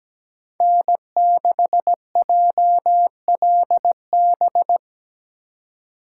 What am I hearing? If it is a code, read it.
Morse code, heard as a keyed tone, about 20 wpm, 700 Hz. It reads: N6JLB
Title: Speed Racing - USA 1x3 Call Signs 17wpm